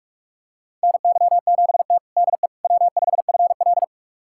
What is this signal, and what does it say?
Morse code, heard as a keyed tone, about 44 wpm, 700 Hz. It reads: NY7T DE W5FL
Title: USA Callsign DE Callsign - 4 Characters - 45wpm